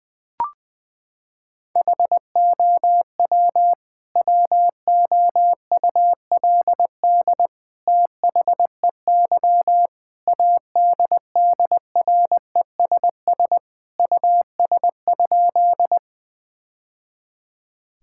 Morse code, heard as a keyed tone, about 20 wpm, 700 Hz. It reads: HOW WOULD THEY ADDRESS US?
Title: Sentences from Top 1000 Words - Review Part 6 - 20wpm